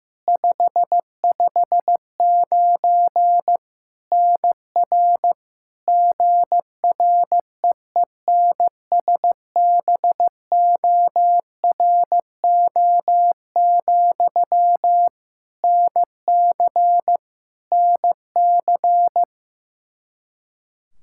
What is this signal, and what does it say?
Morse code, heard as a keyed tone, about 15 wpm, 700 Hz. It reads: 559 NR GREENSBORO, NC NC
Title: K1USN SST QSO Practice 15wpm